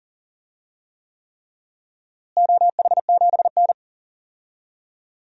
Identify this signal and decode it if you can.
Morse code, heard as a keyed tone, about 40 wpm, 700 Hz. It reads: OH7N
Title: DXCC-100 Call Signs - 4 Characters - Repeated 3x - 40wpm